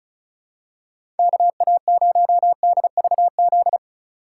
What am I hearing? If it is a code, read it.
Morse code, heard as a keyed tone, about 34 wpm, 700 Hz. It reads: KA0DVZ
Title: USA Call Signs - 6 Characters - 4x Character Spacing - 35wpm